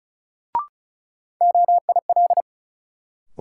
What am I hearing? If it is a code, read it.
Morse code, heard as a keyed tone, about 36 wpm, 700 Hz. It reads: OIL
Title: Sentences from Top 1000 Words - Review Part 4 - New Word Drill - 35wpm